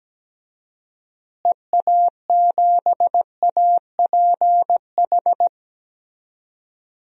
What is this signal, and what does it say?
Morse code, heard as a keyed tone, about 17 wpm, 700 Hz. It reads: EA7APH
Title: DXCC-100 Call Signs - 6 Characters - Repeated 3x - 17wpm